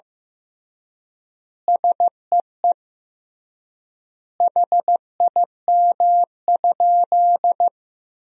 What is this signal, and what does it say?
Morse code, heard as a keyed tone, about 15 wpm, 700 Hz. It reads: SEE HIM?
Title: 3x Word-Spacing - Sentences from Top 100 Words - 15wpm